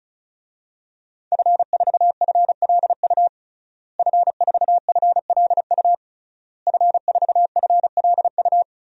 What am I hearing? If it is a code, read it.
Morse code, heard as a keyed tone, about 36 wpm, 700 Hz. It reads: F4FLU F4FLU F4FLU
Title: DXCC-100 Call Signs - 5 Characters - Repeated 3x - 35wpm